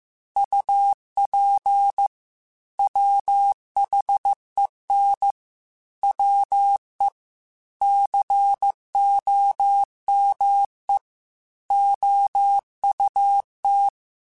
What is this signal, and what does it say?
Morse code, heard as a keyed tone, about 15 wpm, 800 Hz. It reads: UP WHEN WE COME OUT
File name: QSOs / QSOs-mp3s / WD0BGP 2@13